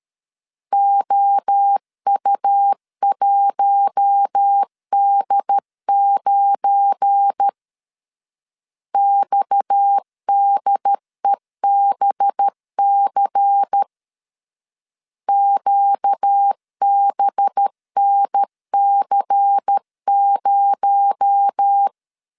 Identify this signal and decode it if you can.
Morse code, heard as a keyed tone, about 13 wpm, 800 Hz. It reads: OU1D9 XDEBC QBNC0
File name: LEZIONI / 7 GF9Z / 7 GF9 12wpm 2319